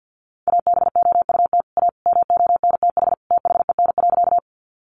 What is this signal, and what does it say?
Morse code, heard as a keyed tone, about 50 wpm, 700 Hz. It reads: ABOUT A MONTH THEN.
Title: Sentences from Top 800 Words 50wpm